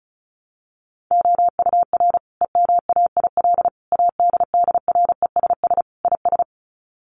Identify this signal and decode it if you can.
Morse code, heard as a keyed tone, about 34 wpm, 700 Hz. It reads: OUR EMAIL ADDRESS IS
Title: Sets of 5 Words Encore2 35wpm